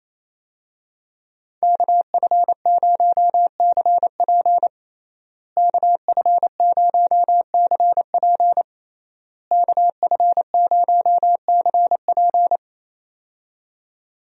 Morse code, heard as a keyed tone, about 28 wpm, 700 Hz. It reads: KF0CP KF0CP KF0CP
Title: USA Call Signs - 5 Characters - Repeated 3x - 28wpm